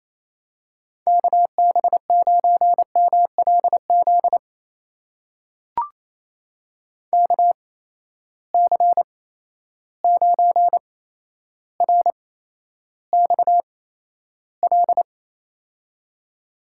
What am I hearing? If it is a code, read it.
Morse code, heard as a keyed tone, about 28 wpm, spaced as for 12 wpm, 700 Hz. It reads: KB9MLZ KC9RXL
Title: USA Call Signs - 8x Character Spacing - 28wpm